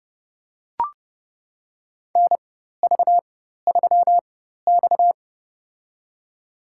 Morse code, heard as a keyed tone, about 31 wpm, spaced as for 14 wpm, 700 Hz. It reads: NV3X
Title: USA Call Signs - 4 Characters - 4x Character Spacing - 30wpm